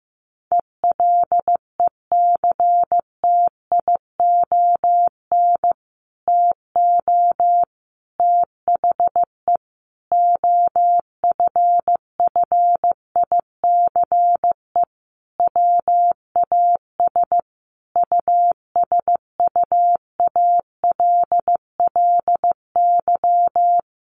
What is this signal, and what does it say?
Morse code, heard as a keyed tone, about 15 wpm, 700 Hz. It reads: ELECTION TO THE OFFICE WAS USUALLY
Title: Sentences from Top 1500 Words 15wpm